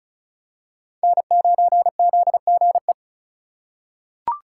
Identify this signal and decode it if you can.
Morse code, heard as a keyed tone, about 36 wpm, 700 Hz. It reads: N9ZGE
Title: USA Call Signs - 5 Characters - 4x Character Spacing - 35wpm